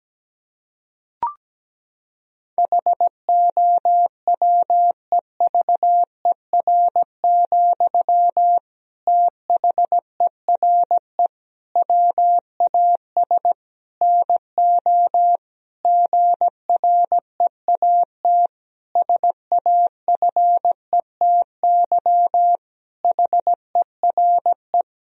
Morse code, heard as a keyed tone, about 17 wpm, 700 Hz. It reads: HOWEVER, THERE WAS NO GREAT SAFETY HERE
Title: Sentences from Top 1000 Words - Review Part 4 - 17wpm